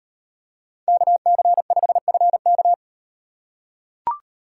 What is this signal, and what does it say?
Morse code, heard as a keyed tone, about 35 wpm, 700 Hz. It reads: KC5FK